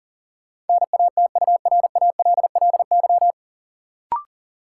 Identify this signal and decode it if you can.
Morse code, heard as a keyed tone, about 40 wpm, 700 Hz. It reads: NATURALLY
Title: Sentences from Top 1300 Words - New Word Drill Alt - 40wpm